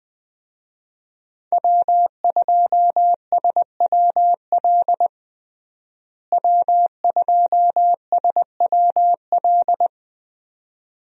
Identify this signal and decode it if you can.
Morse code, heard as a keyed tone, about 20 wpm, 700 Hz. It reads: W2SWL W2SWL
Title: USA Call Signs - 5 Characters - Repeated 2x - 20wpm